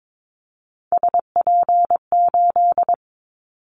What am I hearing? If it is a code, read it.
Morse code, heard as a keyed tone, about 22 wpm, 700 Hz. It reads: SP8
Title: Callsign Prefixes for Top 25 DXCC - 22wpm